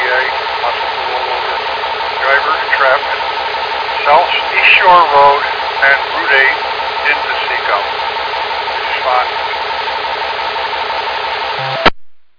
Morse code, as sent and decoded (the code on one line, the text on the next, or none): none